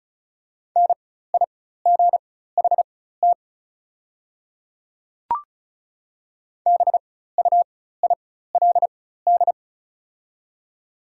Morse code, heard as a keyed tone, about 34 wpm, spaced as for 16 wpm, 690 Hz. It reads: NIGHT BUILD
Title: 4x Character Spacing - Top 500 Words - 35wpm